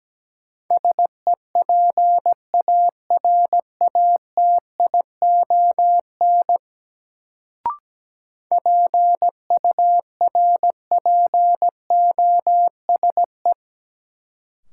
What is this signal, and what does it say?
Morse code, heard as a keyed tone, about 17 wpm, 700 Hz. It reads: SEPARATION PURPOSE
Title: Sets of 5 Words - New Word Drill - 17wpm